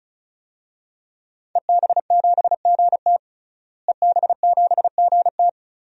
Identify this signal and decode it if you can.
Morse code, heard as a keyed tone, about 34 wpm, 700 Hz. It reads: EB7GT EB7GT
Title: DXCC-100 Call Signs - 5 Characters - Repeated 2x - 35wpm